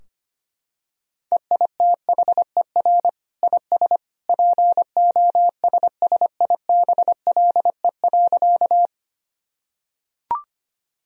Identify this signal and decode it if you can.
Morse code, heard as a keyed tone, about 25 wpm, 700 Hz. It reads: EITHER IS POSSIBLE.
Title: Sentences from Top 1000 Words - Review Part 4 - 25wpm